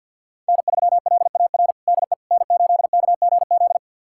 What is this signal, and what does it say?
Morse code, heard as a keyed tone, about 48 wpm, 700 Hz. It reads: N3LAR DE N8KGZ
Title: USA Callsign DE Callsign - 5 Characters - 50wpm